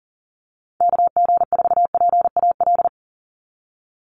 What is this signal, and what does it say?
Morse code, heard as a keyed tone, about 40 wpm, 700 Hz. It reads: KG4PAL
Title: USA Call Signs 40wpm